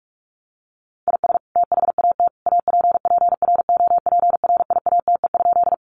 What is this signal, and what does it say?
Morse code, heard as a keyed tone, about 44 wpm, 700 Hz. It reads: IS THAT APPROPRIATE?
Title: Sentences from Top 900 Words 45wpm